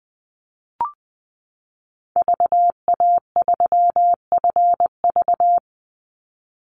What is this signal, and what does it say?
Morse code, heard as a keyed tone, about 20 wpm, 700 Hz. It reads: VA3FV
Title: Call Signs Repeated 2x - 20wpm